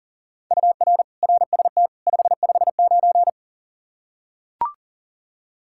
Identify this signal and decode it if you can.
Morse code, heard as a keyed tone, about 37 wpm, 700 Hz. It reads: UR RST 559
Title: Practice QSO - Exchange 1 - Beginning - 40wpm